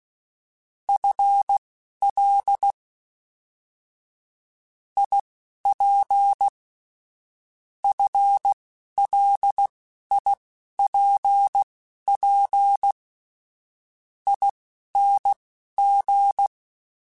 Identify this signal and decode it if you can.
Morse code, heard as a keyed tone, about 16 wpm, spaced as for 11 wpm, 800 Hz. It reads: FL IP FLIPP ING
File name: FL / FL mp3s / Fast FL @10